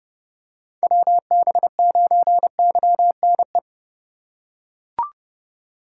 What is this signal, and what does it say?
Morse code, heard as a keyed tone, about 30 wpm, 700 Hz. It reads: WB9YNE